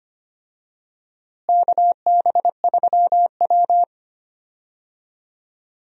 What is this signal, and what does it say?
Morse code, heard as a keyed tone, about 25 wpm, 700 Hz. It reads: KB3W
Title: USA Call Signs - 4 Characters - Repeated 2x - 25wpm